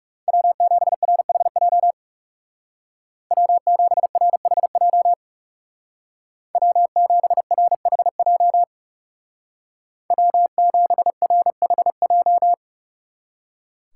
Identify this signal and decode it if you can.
Morse code, heard as a keyed tone, about 36 wpm, 700 Hz. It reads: W7RHJ W7RHJ W7RHJ W7RHJ
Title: Speed Racing - USA 1x3 Call Signs 30wpm